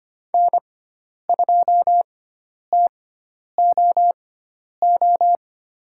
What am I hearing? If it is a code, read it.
Morse code, heard as a keyed tone, about 25 wpm, spaced as for 10 wpm, 700 Hz. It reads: N2TOO
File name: course / 25:10wpm / 095 Lesson 16g - Letter c - Call Signs 25wpm